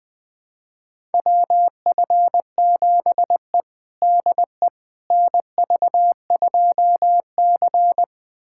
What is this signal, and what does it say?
Morse code, heard as a keyed tone, about 21 wpm, 700 Hz. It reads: WF7E DE NV2C
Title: USA Callsign DE Callsign - 4 Characters - 20wpm